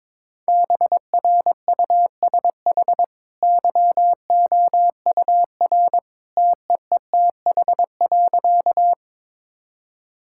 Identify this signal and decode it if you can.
Morse code, heard as a keyed tone, about 22 wpm, 690 Hz. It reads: BRUSH YOUR TEETH.
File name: practice / 22wpm / Easy Sentences with Hints - 22wpm